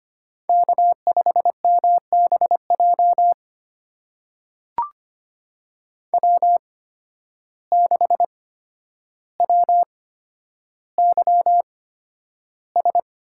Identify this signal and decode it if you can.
Morse code, heard as a keyed tone, about 25 wpm, spaced as for 11 wpm, 700 Hz. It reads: K5MBJ W6WYS
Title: USA Call Signs - 5 Characters - 8x Character Spacing - 25wpm